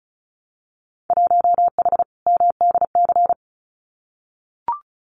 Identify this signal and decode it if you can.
Morse code, heard as a keyed tone, about 36 wpm, 700 Hz. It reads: 1H MDC
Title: Winter Field Day Exchanges 35wpm